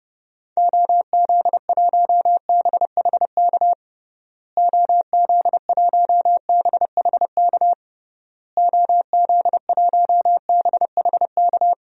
Morse code, heard as a keyed tone, about 31 wpm, 700 Hz. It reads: OZ1BHK OZ1BHK OZ1BHK
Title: DXCC-100 Call Signs - 6 Characters - Repeated 3x - 30wpm